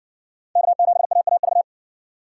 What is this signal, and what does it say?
Morse code, heard as a keyed tone, about 58 wpm, 700 Hz. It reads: K7AAV